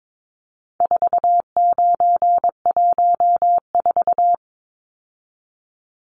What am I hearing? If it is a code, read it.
Morse code, heard as a keyed tone, about 22 wpm, 700 Hz. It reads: 4914
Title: Numbers - 4-Digit 22wpm